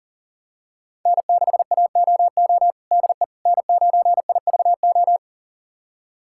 Morse code, heard as a keyed tone, about 40 wpm, 700 Hz. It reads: N6AOO DE N9IVO
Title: USA Callsign DE Callsign - 5 Characters - 40wpm